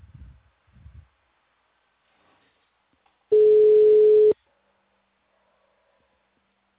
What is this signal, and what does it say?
Ringback tone (Europe, 425 Hz).